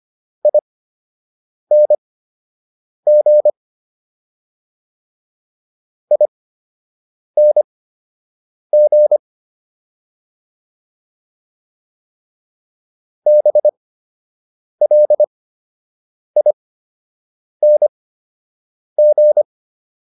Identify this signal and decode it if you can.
Morse code, heard as a keyed tone, about 25 wpm, spaced as for 7 wpm, 600 Hz. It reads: ING ING BLING